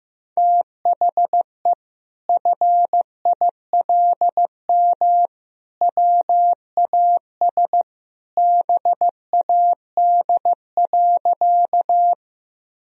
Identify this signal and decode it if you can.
Morse code, heard as a keyed tone, about 15 wpm, 700 Hz. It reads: THE FILM WAS BAD.
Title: Sentences from Top 1000 Words - Review Part 3 - 15wpm